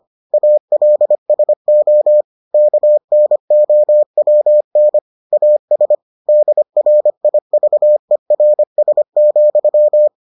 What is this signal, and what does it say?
Morse code, heard as a keyed tone, about 25 wpm, 600 Hz. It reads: ALSO KNOWN AS DRIVERS,